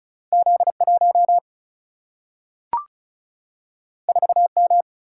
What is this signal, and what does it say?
Morse code, heard as a keyed tone, about 36 wpm, 700 Hz. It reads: Z1 4M
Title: Single-Letter Single-Number 35wpm